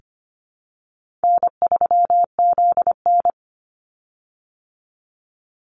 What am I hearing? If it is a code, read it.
Morse code, heard as a keyed tone, about 25 wpm, 700 Hz. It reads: N3ZN
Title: Easy Call Signs Repeated 2x - 25wpm